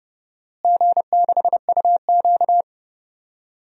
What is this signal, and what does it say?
Morse code, heard as a keyed tone, about 30 wpm, 700 Hz. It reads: G6UQ
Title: DXCC-100 Call Signs - 4 Characters - Repeated 2x - 30wpm